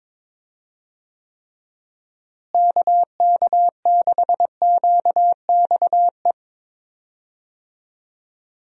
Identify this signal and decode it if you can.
Morse code, heard as a keyed tone, about 22 wpm, 700 Hz. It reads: KK6QXE
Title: USA Call Signs - 6 Characters - Repeated 2x - 22wpm